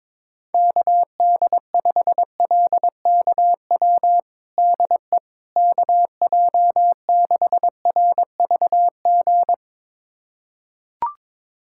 Morse code, heard as a keyed tone, about 22 wpm, 700 Hz. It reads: KD5LKW DE KJ6RVG
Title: USA Callsign DE Callsign - 6 Characters - 22wpm